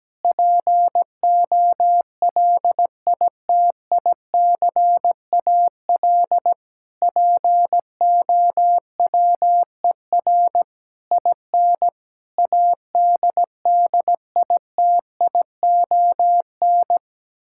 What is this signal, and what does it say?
Morse code, heard as a keyed tone, about 17 wpm, 700 Hz. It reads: POLITICAL POWER IN ADDITION
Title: Sentences from Top 1000 Words - Review Part 6 - 17wpm